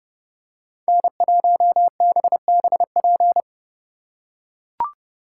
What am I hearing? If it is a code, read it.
Morse code, heard as a keyed tone, about 28 wpm, 700 Hz. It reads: N1BBP